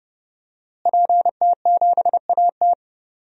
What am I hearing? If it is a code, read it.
Morse code, heard as a keyed tone, about 30 wpm, 700 Hz. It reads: PT7AT
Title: DXCC-100 Call Signs - 5 Characters - Repeated 2x - 30wpm